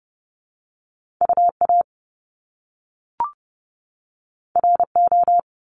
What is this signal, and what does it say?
Morse code, heard as a keyed tone, about 30 wpm, 700 Hz. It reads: UA RO
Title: Top Two Letter Combinations in Words 30wpm